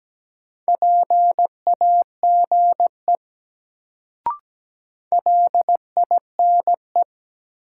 Morse code, heard as a keyed tone, about 17 wpm, 700 Hz. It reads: PAGE LINE
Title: Sentences from Top 1000 Words - Review Part 1 - New Word Drill - 17wpm